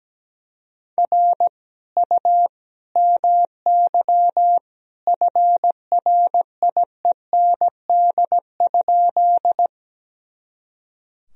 Morse code, heard as a keyed tone, about 17 wpm, 700 Hz. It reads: R U MY FRIEND?
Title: Sentences from Top 300 Words and Abbreviations 17wpm